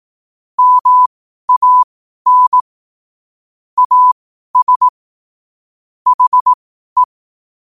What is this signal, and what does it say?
Morse code, heard as a keyed tone, about 18 wpm, spaced as for 12 wpm, 1000 Hz. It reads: MAN AS HE
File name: qotdfiles / 12wpm